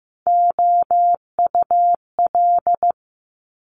Morse code, heard as a keyed tone, about 15 wpm, 700 Hz. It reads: OUL
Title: Multi-letter Phonemes Quick Fire 15wpm